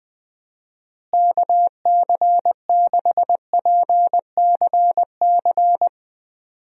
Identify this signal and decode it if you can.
Morse code, heard as a keyed tone, about 20 wpm, 700 Hz. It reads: KC6PCC
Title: USA Call Signs - 6 Characters - Repeated 3x - 20wpm